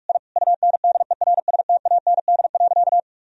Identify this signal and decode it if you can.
Morse code, heard as a keyed tone, about 45 wpm, 700 Hz. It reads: I UNDERSTAND.
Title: Sentences from Top 1000 Words - Review Part 4 - 45wpm